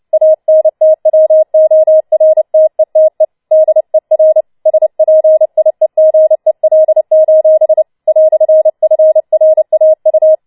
Morse code, heard as a keyed tone, about 29 wpm, 630 Hz. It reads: ANTWORTETE DER SPIEGEL: "FRAU